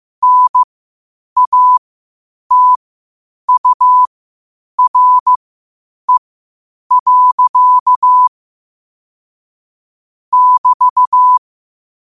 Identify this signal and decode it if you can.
Morse code, heard as a keyed tone, about 15 wpm, spaced as for 8 wpm, 1000 Hz. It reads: NATURE. <BT>
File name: qotdfiles / 8wpm